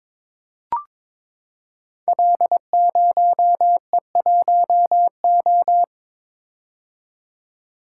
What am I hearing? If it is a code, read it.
Morse code, heard as a keyed tone, about 22 wpm, 700 Hz. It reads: L0E1O
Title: Code Groups - 5 Letter:Numbers - 22wpm